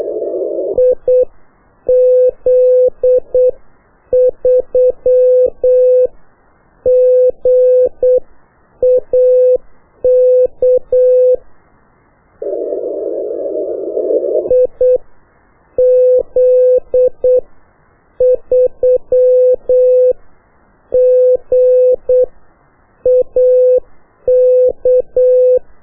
Morse code, heard as a keyed tone, about 8 wpm, spaced as for 7 wpm, 500 Hz. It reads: IZ3GAK IZ3GAK
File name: qso1-qrs